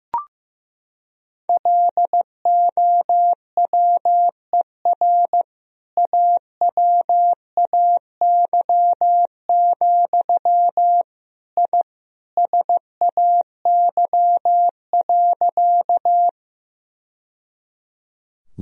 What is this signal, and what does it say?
Morse code, heard as a keyed tone, about 15 wpm, 700 Hz. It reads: LOWER AWAY, I SAY.
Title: Sentences from Top 1000 Words - Review Part 4 - 15wpm